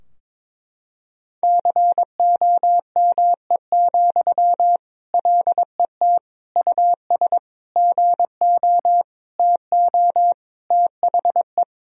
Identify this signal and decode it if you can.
Morse code, heard as a keyed tone, about 22 wpm, 700 Hz. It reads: COME, LET US GO TO THE
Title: Sentences from Top 1000 Words - Review Part 4 - 22wpm